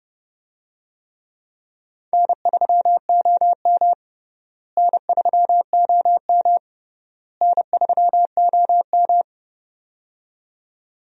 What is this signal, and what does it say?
Morse code, heard as a keyed tone, about 30 wpm, 700 Hz. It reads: N3OM N3OM N3OM
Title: USA Call Signs - 4 Characters - Repeated 3x - 30wpm